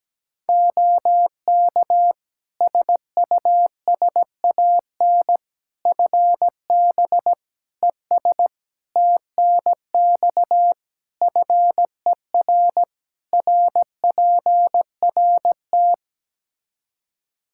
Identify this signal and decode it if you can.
Morse code, heard as a keyed tone, about 17 wpm, 700 Hz. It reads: OK SUSAN FB ES TNX FER RPRT
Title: Practice QSO - Exchange 2 - Info - 17wpm